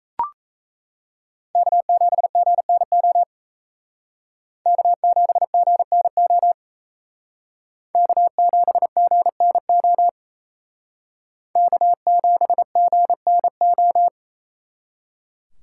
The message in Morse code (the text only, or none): K7GNO K7GNO K7GNO K7GNO